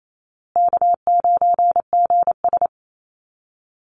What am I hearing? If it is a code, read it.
Morse code, heard as a keyed tone, about 28 wpm, 700 Hz. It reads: K9GS
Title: Easy Call Signs Repeated 3x - 28wpm